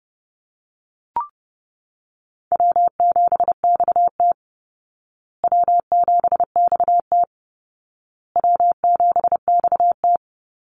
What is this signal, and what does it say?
Morse code, heard as a keyed tone, about 30 wpm, 700 Hz. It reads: W7XT W7XT W7XT
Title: Call Signs Repeated 2x - 30wpm